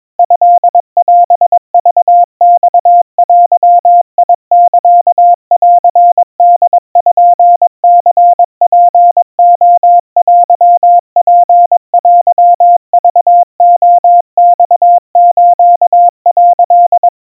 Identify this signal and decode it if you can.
Morse code (Japanese Wabun code), heard as a keyed tone, about 22 wpm, 700 Hz. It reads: トオクマデサンポニツレテツテクレマス」
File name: 80 cpm 700 Hz Crispy